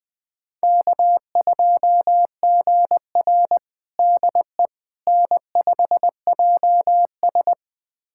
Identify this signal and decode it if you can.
Morse code, heard as a keyed tone, about 20 wpm, 700 Hz. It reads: K2GR DE N5JS
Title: USA Callsign DE Callsign - 4 Characters - 20wpm